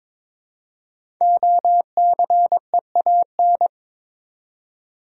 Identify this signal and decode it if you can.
Morse code, heard as a keyed tone, about 22 wpm, 700 Hz. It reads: OCEAN
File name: 5 Letter Words - First 3 Letters Given - 22wpm